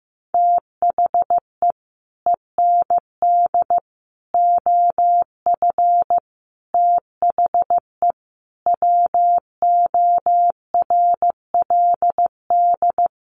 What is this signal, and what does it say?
Morse code, heard as a keyed tone, about 15 wpm, 700 Hz. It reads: THE END OF THE WORLD
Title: Sets of 5 Words 15wpm